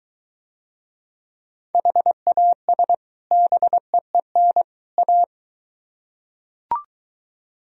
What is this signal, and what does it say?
Morse code, heard as a keyed tone, about 22 wpm, 700 Hz. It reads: HAS BEEN A